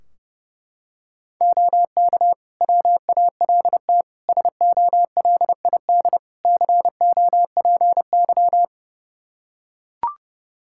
Morse code, heard as a keyed tone, about 30 wpm, 700 Hz. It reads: OK WALT SOLID COPY
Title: Practice QSO - Exchange 3 - More Info - 30wpm